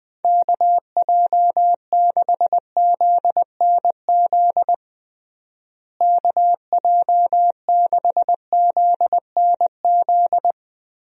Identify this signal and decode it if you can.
Morse code, heard as a keyed tone, about 20 wpm, 700 Hz. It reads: KJ6ZNZ KJ6ZNZ
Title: USA Call Signs - 6 Characters - Repeated 3x - 20wpm